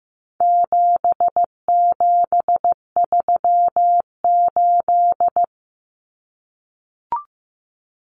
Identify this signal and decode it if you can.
Morse code, heard as a keyed tone, about 15 wpm, 700 Hz. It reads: 7738
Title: Numbers - 4-Digit 15wpm